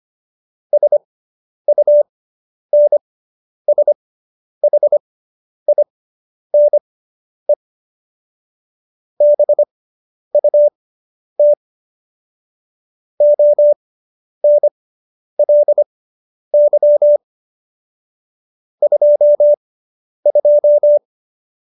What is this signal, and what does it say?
Morse code, heard as a keyed tone, about 25 wpm, spaced as for 10 wpm, 590 Hz. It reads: SUNSHINE BUT ONLY 22